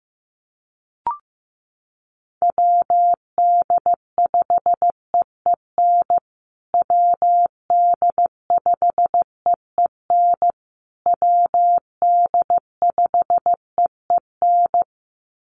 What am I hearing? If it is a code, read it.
Morse code, heard as a keyed tone, about 15 wpm, 700 Hz. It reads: WD5EEN WD5EEN WD5EEN
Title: USA Call Signs Repeated 3x 15wpm